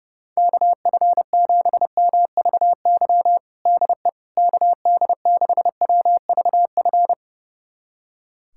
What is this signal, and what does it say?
Morse code, heard as a keyed tone, about 31 wpm, 700 Hz. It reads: KF7MVY DE KD6WVF
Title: USA Callsign DE Callsign - 6 Characters - 30wpm